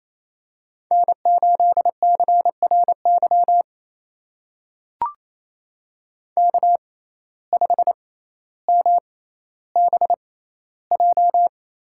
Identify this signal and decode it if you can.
Morse code, heard as a keyed tone, about 28 wpm, spaced as for 14 wpm, 700 Hz. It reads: N8CRY K5MBJ
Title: USA Call Signs - 5 Characters - 6x Character Spacing - 28wpm